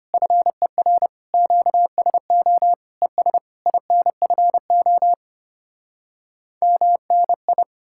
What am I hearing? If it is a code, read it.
Morse code, heard as a keyed tone, about 33 wpm, 700 Hz. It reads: FER QSO ES INFO MNI